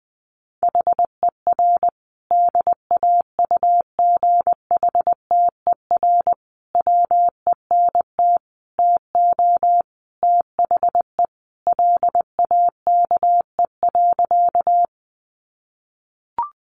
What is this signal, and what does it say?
Morse code, heard as a keyed tone, about 20 wpm, 700 Hz. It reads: HER DAUGHTER WENT TO THE LAKE.
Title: Sentences from Top 1500 Words 20wpm